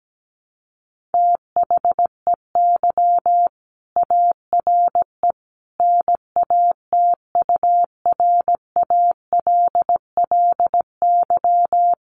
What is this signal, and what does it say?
Morse code, heard as a keyed tone, about 17 wpm, 700 Hz. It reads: THEY ARE NATURALLY
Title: Sentences from Top 1300 Words 17wpm